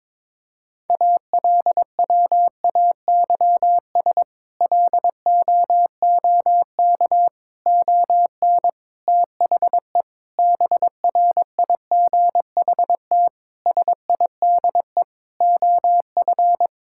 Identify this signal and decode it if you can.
Morse code, heard as a keyed tone, about 21 wpm, 700 Hz. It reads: ALWAYS LOOK ON THE BRIGHT SIDE OF
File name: practice / 22wpm / Sets of 8 Words - 22wpm